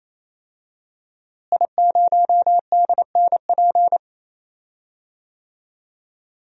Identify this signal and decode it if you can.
Morse code, heard as a keyed tone, about 28 wpm, 700 Hz. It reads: I0DNP
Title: DXCC-100 Call Signs - 5 Characters - Repeated 3x - 28wpm